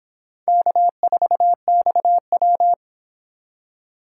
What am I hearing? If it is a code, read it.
Morse code, heard as a keyed tone, about 25 wpm, 700 Hz. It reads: K4XW